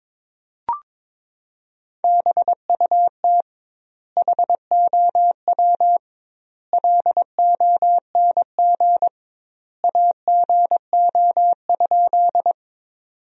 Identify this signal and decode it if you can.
Morse code, heard as a keyed tone, about 21 wpm, 700 Hz. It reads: BUT HOW LONG AGO?
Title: 2x Word-Spacing - Sentences from Top 500 Words - 22wpm